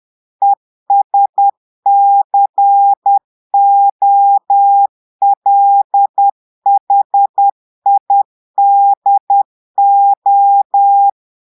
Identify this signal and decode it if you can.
Morse code, heard as a keyed tone, about 10 wpm, 800 Hz. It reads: ESCOLHIDO
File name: Portugais - texte en clair 1 - vitesse - 10 mots minute